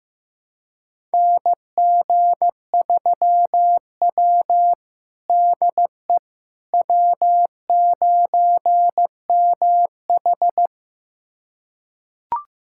Morse code, heard as a keyed tone, about 15 wpm, 700 Hz. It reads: NG3W DE W9MH
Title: USA Callsign DE Callsign - 4 Characters - 15wpm